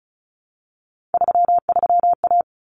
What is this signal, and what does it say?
Morse code, heard as a keyed tone, about 34 wpm, 700 Hz. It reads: 33A
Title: Common Cut Numbers - 3-Digit 35wpm